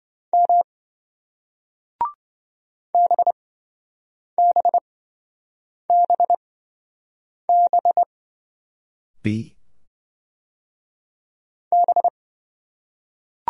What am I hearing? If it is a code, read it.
Morse code, heard as a keyed tone, about 27 wpm, spaced as for 7 wpm, 700 Hz. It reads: M BBBB B